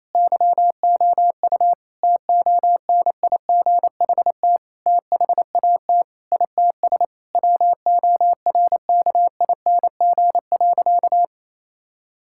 Morse code, heard as a keyed tone, about 28 wpm, 700 Hz. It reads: YOU TONIGHT THAT ITS WORKING.
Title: Sentences from Top 1000 Words - Review Part 6 - 28wpm